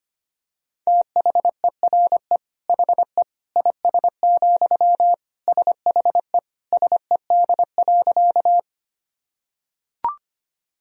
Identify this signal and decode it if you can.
Morse code, heard as a keyed tone, about 25 wpm, 700 Hz. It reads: THERE HE IS, SHE SED.
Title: Sentences from Top 100 Words and Abbreviations 25wpm